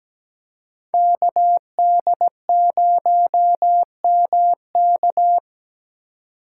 Morse code, heard as a keyed tone, about 17 wpm, 700 Hz. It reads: KD0MK
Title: USA Call Signs - 8x Character Spacing - 17wpm